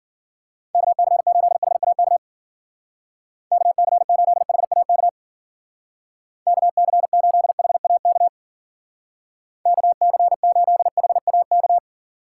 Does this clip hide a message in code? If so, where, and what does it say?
Yes: Morse code from 0.8 to 11.8 s, heard as a keyed tone, about 53 wpm, 700 Hz. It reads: KC8HAK KC8HAK KC8HAK KC8HAK